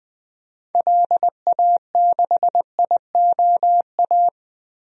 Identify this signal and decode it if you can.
Morse code, heard as a keyed tone, about 21 wpm, 700 Hz. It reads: LA6IOA
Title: DXCC-100 Call Signs - 6 Characters - 20wpm